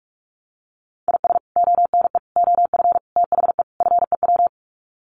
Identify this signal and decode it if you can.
Morse code, heard as a keyed tone, about 44 wpm, 700 Hz. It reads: IS ONE OF THE FEW
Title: Sets of 5 Words 45wpm